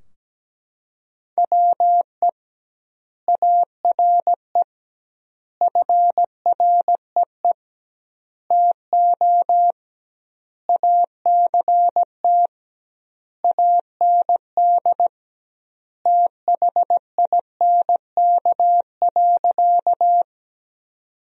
Morse code, heard as a keyed tone, about 17 wpm, 700 Hz. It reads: WE ARE FREE TO ACT AND THINK.
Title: 2x Word-Spacing - Sentences from Top 500 Words - 17wpm